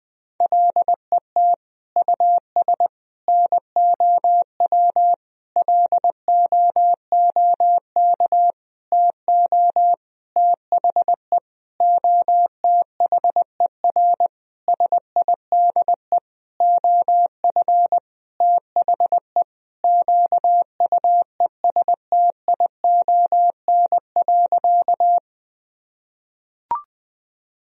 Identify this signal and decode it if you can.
Morse code, heard as a keyed tone, about 20 wpm, 700 Hz. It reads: LET US NOW LOOK TO THE OTHER SIDE OF THE QUESTION.
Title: Sentences from Top 1000 Words - Review Part 1 - 20wpm